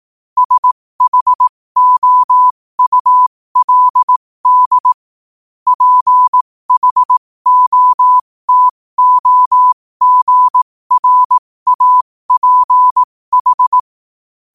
Morse code, heard as a keyed tone, about 18 wpm, spaced as for 15 wpm, 1000 Hz. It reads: SHOULD PHOTOGRAPH
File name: qotdfiles / 15wpm